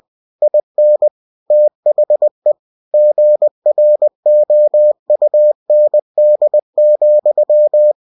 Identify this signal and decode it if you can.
Morse code, heard as a keyed tone, about 20 wpm, 600 Hz. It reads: IN THE GROUND,
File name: cw-books / winnie-the-pooh / winnie-20wpm-page-0085